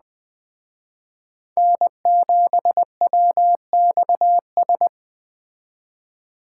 Morse code, heard as a keyed tone, about 20 wpm, 700 Hz. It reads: N7WXS
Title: Speed Racing - USA Call Signs 17wpm